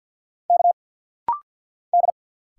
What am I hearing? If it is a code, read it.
Morse code, heard as a keyed tone, about 50 wpm, 700 Hz. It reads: K D
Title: Single Letters - Rapid-Fire 50wpm